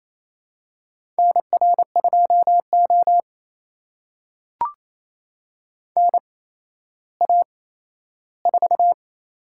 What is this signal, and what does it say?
Morse code, heard as a keyed tone, about 28 wpm, spaced as for 14 wpm, 700 Hz. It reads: NR2O NA4
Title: USA Call Signs - 4 Characters - 8x Character Spacing - 28wpm